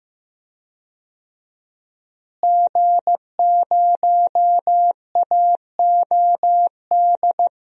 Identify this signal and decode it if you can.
Morse code, heard as a keyed tone, about 15 wpm, 700 Hz. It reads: G0AOD
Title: DXCC-100 Call Signs - 5 Characters - Repeated 2x - 15wpm